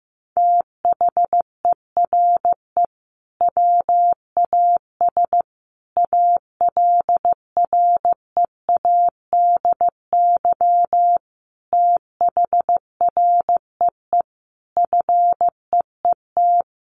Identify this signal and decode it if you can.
Morse code, heard as a keyed tone, about 15 wpm, 700 Hz. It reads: THERE WAS ALREADY THREE FEET
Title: Sentences from Top 700 Words 15wpm